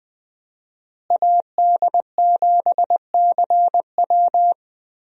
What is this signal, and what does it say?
Morse code, heard as a keyed tone, about 20 wpm, 700 Hz. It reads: AD7CW
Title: USA Call Signs - 5 Characters - Repeated 3x - 20wpm